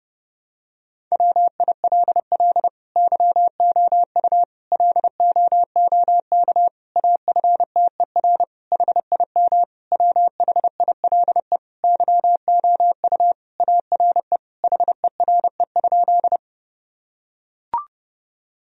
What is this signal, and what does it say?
Morse code, heard as a keyed tone, about 30 wpm, 700 Hz. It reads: WILL YOU LOOK AFTER HIM WHILE YOU ARE HERE?
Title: Sentences from Top 1000 Words - Review Part 1 - 30wpm